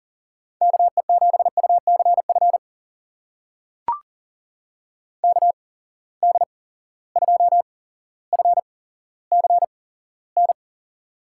Morse code, heard as a keyed tone, about 39 wpm, spaced as for 17 wpm, 700 Hz. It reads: KE7UCF KD2FCN
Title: USA Call Signs - 6 Characters - 8x Character Spacing - 40wpm